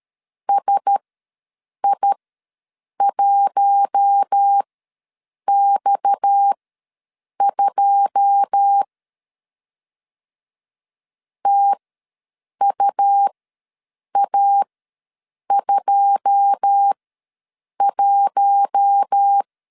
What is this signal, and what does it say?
Morse code, heard as a keyed tone, about 13 wpm, spaced as for 7 wpm, 800 Hz. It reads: SI1X2 TUA21
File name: LEZIONI / 9 2 ws 12wpm 3406